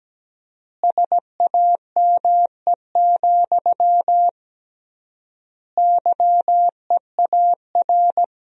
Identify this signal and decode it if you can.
Morse code, heard as a keyed tone, about 17 wpm, 700 Hz. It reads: SAME, YEAR
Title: 3x Word-Spacing - Sentences from Top 300 Words - 17wpm